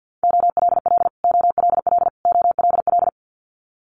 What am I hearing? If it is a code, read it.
Morse code, heard as a keyed tone, about 50 wpm, 700 Hz. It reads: OLL OLL OLL
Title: Multi-letter Phonemes Repeated 3x 50wpm